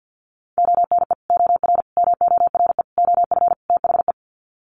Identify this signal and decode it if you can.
Morse code, heard as a keyed tone, about 50 wpm, 700 Hz. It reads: ONE OR MORE OF THE
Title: Sets of 5 Words Encore 50wpm